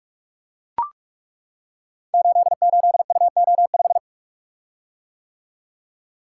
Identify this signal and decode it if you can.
Morse code, heard as a keyed tone, about 42 wpm, 700 Hz. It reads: 88UO5
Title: Code Groups - 5 Letter:Numbers - 45wpm